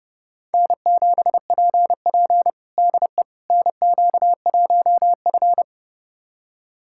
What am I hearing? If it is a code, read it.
Morse code, heard as a keyed tone, about 30 wpm, 700 Hz. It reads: N7PP DE NQ1F
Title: USA Callsign DE Callsign - 4 Characters - 30wpm